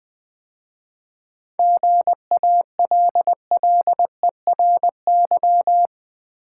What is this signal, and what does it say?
Morse code, heard as a keyed tone, about 21 wpm, 700 Hz. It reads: GALLERY
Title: 7 Letter Words - First 3 Letters Given - 20wpm